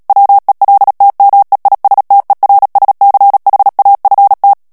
Morse code, heard as a keyed tone, about 37 wpm, 800 Hz. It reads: WELTMEISTERSCHAFT
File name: morse mp3 / ALLEMAND texte clair 2 vitesse 35 mots minute